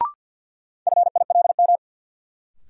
Morse code, heard as a keyed tone, about 46 wpm, 700 Hz. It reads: FILM
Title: Sentences from Top 1000 Words - Review Part 3 - New Word Drill - 50wpm